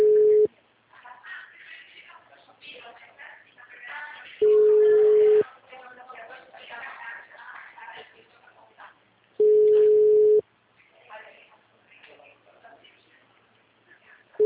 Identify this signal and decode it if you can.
Ringback tone (Europe, 425 Hz), with speech.
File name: out- -2036- - - .53558